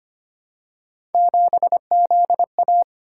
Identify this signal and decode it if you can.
Morse code, heard as a keyed tone, about 25 wpm, 700 Hz. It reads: 7ZA
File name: Three Letter-Number Combinations - 25wpm